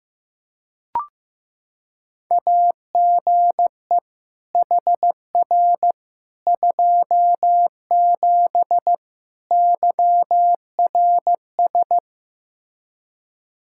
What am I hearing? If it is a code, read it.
Morse code, heard as a keyed tone, about 15 wpm, 700 Hz. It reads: AGE HR 27 YRS
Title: Practice QSO - Exchange 3 - More Info - 15wpm